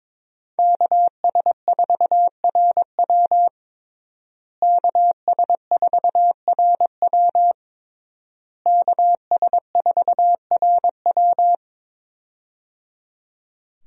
Morse code, heard as a keyed tone, about 22 wpm, 700 Hz. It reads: KS4RW KS4RW KS4RW
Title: USA Call Signs - 5 Characters - Repeated 3x - 22wpm